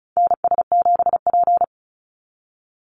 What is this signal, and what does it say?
Morse code, heard as a keyed tone, about 34 wpm, 700 Hz. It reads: NS7P
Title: NA SOTA Activator Callsigns 35wpm